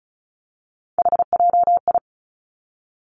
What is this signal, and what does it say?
Morse code, heard as a keyed tone, about 36 wpm, 700 Hz. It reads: HJI
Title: Three Challenging Letter-Number Combinations 35wpm